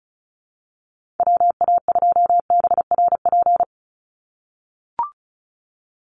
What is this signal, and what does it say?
Morse code, heard as a keyed tone, about 34 wpm, 700 Hz. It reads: WA2BRP
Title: USA Call Signs 35wpm